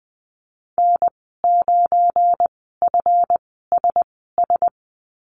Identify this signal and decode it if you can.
Morse code, heard as a keyed tone, about 20 wpm, spaced as for 14 wpm, 700 Hz. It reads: N9FSS
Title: 179 Lesson 29g - Slash - Call Signs 20wpm